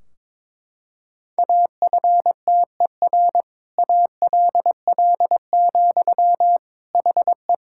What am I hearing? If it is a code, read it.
Morse code, heard as a keyed tone, about 22 wpm, 700 Hz. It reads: AFTER ALL, HE
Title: Sentences from Top 1600 Words 22wpm